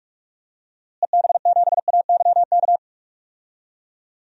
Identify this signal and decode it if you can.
Morse code, heard as a keyed tone, about 46 wpm, 700 Hz. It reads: EB7AYK
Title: DXCC-100 Call Signs - 6 Characters - Repeated 3x - 45wpm